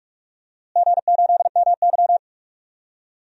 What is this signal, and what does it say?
Morse code, heard as a keyed tone, about 46 wpm, 700 Hz. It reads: G8MY
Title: DXCC-100 Call Signs - 4 Characters - Repeated 3x - 45wpm